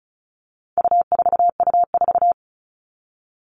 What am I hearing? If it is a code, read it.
Morse code, heard as a keyed tone, about 34 wpm, 700 Hz. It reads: U4U4
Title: All Cut Numbers - 4-Digit 35wpm